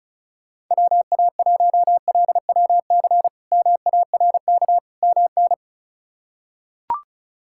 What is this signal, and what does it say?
Morse code, heard as a keyed tone, about 34 wpm, 700 Hz. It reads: WA1LWC MARK MN
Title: K1USN SST QSO Practice - 35wpm